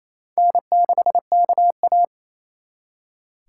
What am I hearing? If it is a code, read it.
Morse code, heard as a keyed tone, about 36 wpm, 700 Hz. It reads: N6KA